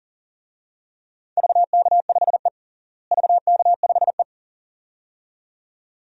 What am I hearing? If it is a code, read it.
Morse code, heard as a keyed tone, about 40 wpm, 700 Hz. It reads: VK5E VK5E
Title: DXCC-100 Call Signs - 4 Characters - Repeated 2x - 40wpm